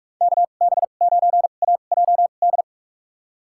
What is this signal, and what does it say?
Morse code, heard as a keyed tone, about 46 wpm, spaced as for 33 wpm, 700 Hz. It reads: KB9AJD
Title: USA Call Signs - 6 Characters - 2x Character Spacing - 45wpm